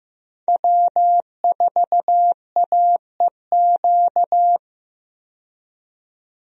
Morse code, heard as a keyed tone, about 15 wpm, 700 Hz. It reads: W4AEQ
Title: USA Call Signs - 5 Characters - 15wpm